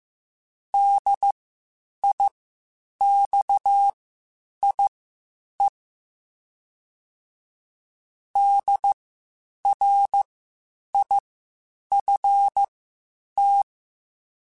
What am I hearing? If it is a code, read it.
Morse code, heard as a keyed tone, about 15 wpm, spaced as for 8 wpm, 800 Hz. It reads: DIXIE DRIFT